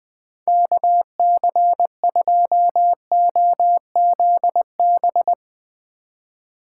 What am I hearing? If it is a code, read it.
Morse code, heard as a keyed tone, about 20 wpm, 700 Hz. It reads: KC2OZB